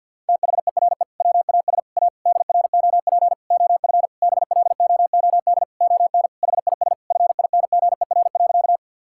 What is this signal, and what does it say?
Morse code, heard as a keyed tone, about 50 wpm, 700 Hz. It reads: THERE WAS A DROP OF BLOOD ON HIS FINGER.
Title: Sentences from Top 1600 Words 50wpm